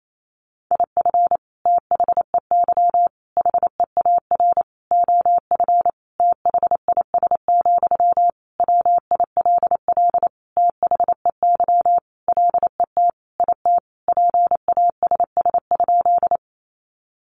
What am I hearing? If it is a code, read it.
Morse code, heard as a keyed tone, about 28 wpm, 700 Hz. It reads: IF THEY HEAR OF THIS, WILL THEY LET IT PASS?
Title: Sentences from Top 400 Words 28wpm